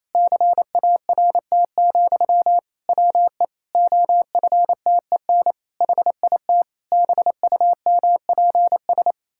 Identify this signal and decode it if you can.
Morse code, heard as a keyed tone, about 28 wpm, 700 Hz. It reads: CART, WE OFTEN HIT BUMPS
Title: Pangram Challenge v2 - 28wpm